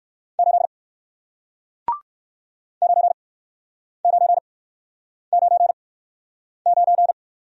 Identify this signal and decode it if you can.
Morse code, heard as a keyed tone, about 60 wpm, 700 Hz. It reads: 8 9 9 9 9